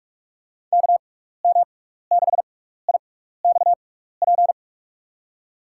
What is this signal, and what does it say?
Morse code, heard as a keyed tone, about 41 wpm, spaced as for 15 wpm, 700 Hz. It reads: KM6IXP
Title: USA Call Signs - 6 Characters - 6x Character Spacing - 45wpm